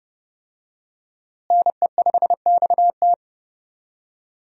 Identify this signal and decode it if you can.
Morse code, heard as a keyed tone, about 32 wpm, 700 Hz. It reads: NE5XT
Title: USA Call Signs - 5 Characters - Repeated 3x - 30wpm